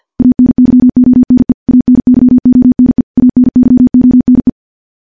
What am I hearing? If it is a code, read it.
Morse code, heard as a keyed tone, about 51 wpm, 250 Hz. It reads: ANYONE ANYONE ANYONE
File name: morse / morse-training-50WPM